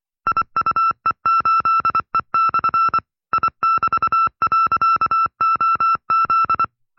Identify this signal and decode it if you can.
Morse code, heard as a keyed tone, about 25 wpm, 1340 Hz. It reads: IUE8E/ I<BT>.OZ